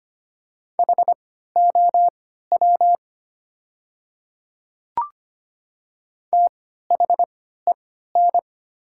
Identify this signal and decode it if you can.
Morse code, heard as a keyed tone, about 25 wpm, spaced as for 14 wpm, 700 Hz. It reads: HOW THEN
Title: 3x Character Spacing - Top 100 Words - 25wpm